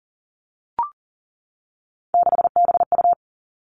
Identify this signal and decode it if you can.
Morse code, heard as a keyed tone, about 38 wpm, 700 Hz. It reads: 6BU
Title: Three Challenging Letter-Number Combinations 40wpm